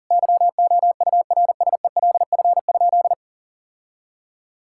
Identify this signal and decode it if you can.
Morse code, heard as a keyed tone, about 40 wpm, 700 Hz. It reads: YOURSELF?
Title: Sentences from Top 1600 Words 40wpm